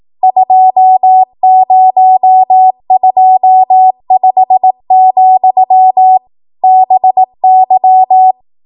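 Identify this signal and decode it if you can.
Morse code, heard as a keyed tone, about 18 wpm, 750 Hz. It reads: 2025, BY
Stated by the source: Code practice at 18 WPM, a transition run bridging 15 to 20 WPM.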